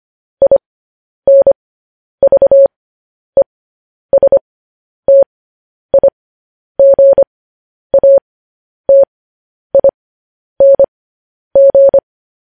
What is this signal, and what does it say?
Morse code, heard as a keyed tone, about 25 wpm, spaced as for 10 wpm, 550 Hz. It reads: INVESTIGATING
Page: news-headlines-in-morse-code-at-10-wpm